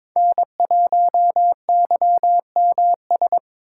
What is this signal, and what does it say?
Morse code, heard as a keyed tone, about 22 wpm, 700 Hz. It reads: N1YMS
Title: USA Call Signs - 5 Characters - 3x Character Spacing - 22wpm